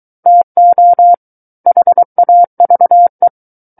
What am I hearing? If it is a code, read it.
Morse code, heard as a keyed tone, about 23 wpm, 700 Hz. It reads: TO HAVE
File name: Book One / 14-Chapter 14 - 23 WPM